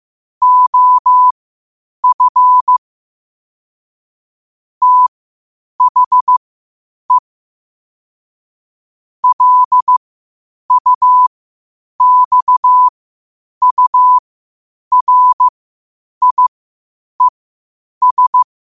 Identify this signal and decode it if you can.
Morse code, heard as a keyed tone, about 15 wpm, spaced as for 8 wpm, 1000 Hz. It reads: OF THE LUXURIES
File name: qotdfiles / 8wpm